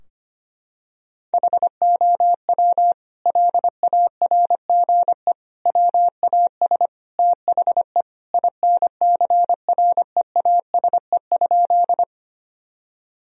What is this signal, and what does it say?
Morse code, heard as a keyed tone, about 25 wpm, 700 Hz. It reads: HOW LARGE WAS THE INCREASE?
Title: Sentences from Top 1000 Words - Review Part 4 - 25wpm